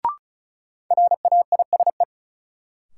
Morse code, heard as a keyed tone, about 36 wpm, 700 Hz. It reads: RAISE
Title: Sentences from Top 1400 Words - New Word Drill Alt - 35wpm